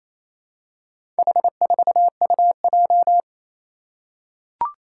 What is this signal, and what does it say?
Morse code, heard as a keyed tone, about 28 wpm, 700 Hz. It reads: H4UJ
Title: Four Challenging Letter-Number Combinations 28wpm